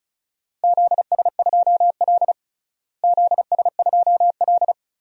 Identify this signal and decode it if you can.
Morse code, heard as a keyed tone, about 34 wpm, 700 Hz. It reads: ZS2L ZS2L
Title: DXCC-100 Call Signs - 4 Characters - Repeated 2x - 35wpm